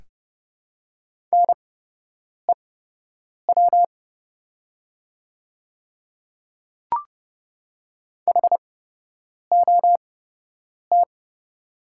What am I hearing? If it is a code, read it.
Morse code, heard as a keyed tone, about 30 wpm, spaced as for 8 wpm, 700 Hz. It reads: NEW HOT
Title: 8x Character Spacing - Top 300 Words - 30wpm